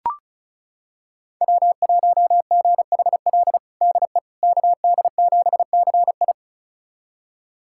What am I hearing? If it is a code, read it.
Morse code, heard as a keyed tone, about 36 wpm, 700 Hz. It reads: W1GHL DE KD7CI
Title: USA Callsign DE Callsign - 5 Characters - 35wpm